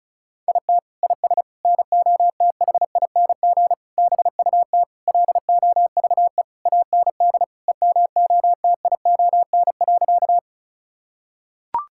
Morse code, heard as a keyed tone, about 36 wpm, 700 Hz. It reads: IT IS NOTHING BUT LOVE AND EMOTION.
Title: Sentences from Top 1600 Words 35wpm